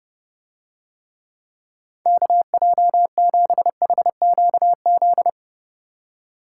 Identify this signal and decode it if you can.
Morse code, heard as a keyed tone, about 29 wpm, 700 Hz. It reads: KJ7HQZ